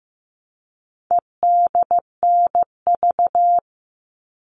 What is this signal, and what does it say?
Morse code, heard as a keyed tone, about 15 wpm, 700 Hz. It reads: EDNV
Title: All Cut Numbers - 4-Digit 15wpm